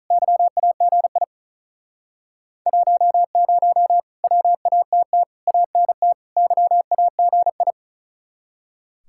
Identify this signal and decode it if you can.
Morse code, heard as a keyed tone, about 45 wpm, 700 Hz. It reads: YAGI 10 WATT ANT YAGI